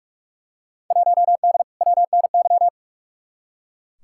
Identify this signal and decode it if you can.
Morse code, heard as a keyed tone, about 43 wpm, 700 Hz. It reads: 1D WNY
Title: Field Day Exchanges - 45wpm